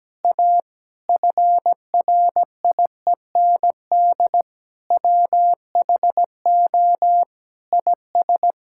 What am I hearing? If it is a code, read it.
Morse code, heard as a keyed tone, about 17 wpm, 700 Hz. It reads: A FRIEND WHO IS
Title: Sentences from Top 1000 Words - Review Part 5 - 17wpm